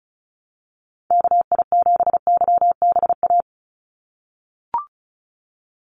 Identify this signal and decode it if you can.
Morse code, heard as a keyed tone, about 34 wpm, 700 Hz. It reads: KI7YBA
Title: USA Call Signs 35wpm